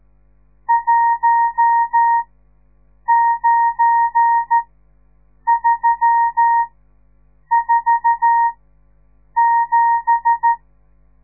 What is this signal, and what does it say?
Morse code, heard as a keyed tone, about 13 wpm, spaced as for 7 wpm, 930 Hz. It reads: 19347
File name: radiomorningstar / morse / les 10